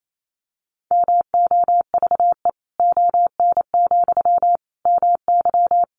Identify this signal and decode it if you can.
Morse code, heard as a keyed tone, about 28 wpm, 700 Hz. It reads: MOVE ON, MY
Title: Sentences from Top 500 Words 28wpm